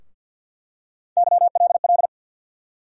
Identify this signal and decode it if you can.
Morse code, heard as a keyed tone, about 50 wpm, 700 Hz. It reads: YLL
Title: USA Call Sign Postfixes - 3 Letters 50wpm